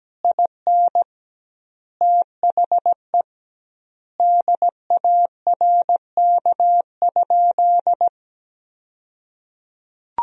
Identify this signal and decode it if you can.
Morse code, heard as a keyed tone, about 17 wpm, 700 Hz. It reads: IN THE DARK?
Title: 2x Word-Spacing - Sentences from Top 500 Words - 17wpm